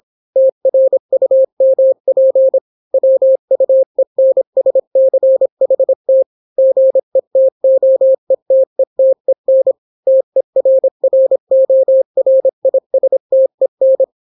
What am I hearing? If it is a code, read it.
Morse code, heard as a keyed tone, about 26 wpm, 530 Hz. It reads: TRUMP WUENSCHT GETOETETEN TERRORISTEN